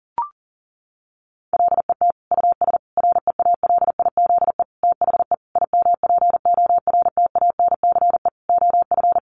Morse code, heard as a keyed tone, about 40 wpm, 700 Hz. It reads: LET US REALIZE THE IMPORTANCE OF
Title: Sentences from Top 1300 Words 40wpm